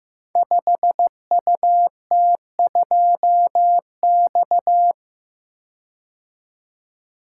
Morse code, heard as a keyed tone, about 15 wpm, 700 Hz. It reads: 5UT2X
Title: Code Groups - 5 Letter:Numbers - 15wpm